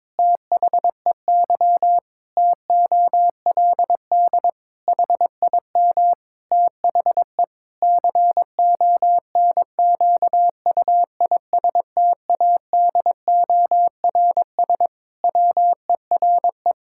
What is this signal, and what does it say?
Morse code, heard as a keyed tone, about 22 wpm, 700 Hz. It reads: THEY TOLD HIM THE CONQUISTADORS WERE
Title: Pangram Challenge v2 - 22wpm